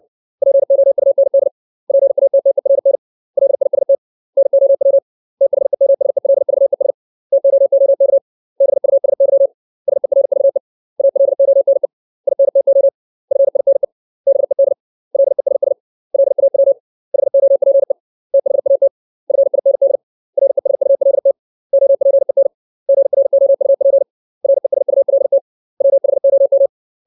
Morse code, heard as a keyed tone, about 60 wpm, 550 Hz. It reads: POUND PATTERN LIST NOW THEMSELVES TOOK BUSY SERVE ALONE INTO FINE BED LESS LAW HOPE THAT FIND RESULT OPEN GROUP RESULT WHOM
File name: 632 words / 632 Words 60WPM